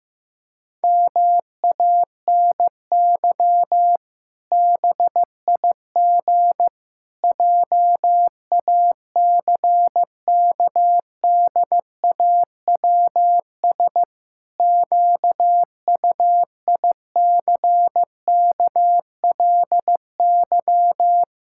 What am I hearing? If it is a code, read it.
Morse code, heard as a keyed tone, about 15 wpm, 700 Hz. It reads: MANY BIG JACKDAWS QUICKLY
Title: Pangram Challenge v2 - 15wpm